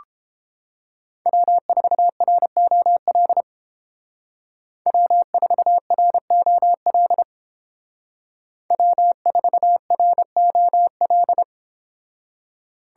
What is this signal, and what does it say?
Morse code, heard as a keyed tone, about 31 wpm, 700 Hz. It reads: W4ROL W4ROL W4ROL